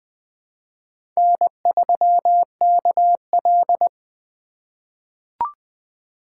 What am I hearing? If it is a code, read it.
Morse code, heard as a keyed tone, about 20 wpm, 700 Hz. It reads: N3KL
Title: USA Call Signs - 4 Characters - 3x Character Spacing - 20wpm